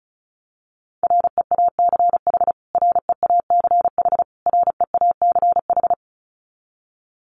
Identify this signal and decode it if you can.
Morse code, heard as a keyed tone, about 34 wpm, 700 Hz. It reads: REACH REACH REACH
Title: Top 500 Words Repeated 35wpm